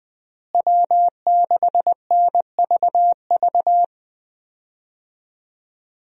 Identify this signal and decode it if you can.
Morse code, heard as a keyed tone, about 20 wpm, 700 Hz. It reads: W6NVV
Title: USA Call Signs - 5 Characters - Repeated 3x - 20wpm